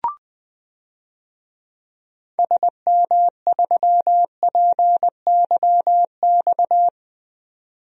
Morse code, heard as a keyed tone, about 20 wpm, 700 Hz. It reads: SM3PYX
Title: DXCC-100 Call Signs - 6 Characters - 20wpm